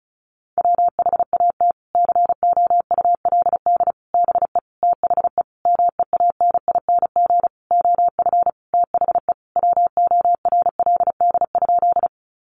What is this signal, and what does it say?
Morse code, heard as a keyed tone, about 34 wpm, 700 Hz. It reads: WHAT COULD BE THE MEANING OF THE WORLD?
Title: Sentences from Top 900 Words 35wpm